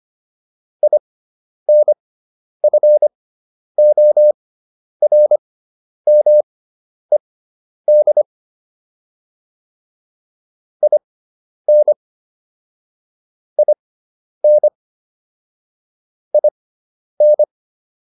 Morse code, heard as a keyed tone, about 25 wpm, spaced as for 10 wpm, 600 Hz. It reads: INFORMED IN IN IN